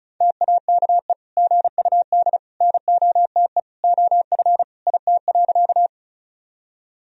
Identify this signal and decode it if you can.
Morse code, heard as a keyed tone, about 36 wpm, 700 Hz. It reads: TAKE GUD NOTE OF IT.
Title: Sentences from Top 500 Words and Abbreviations 35wpm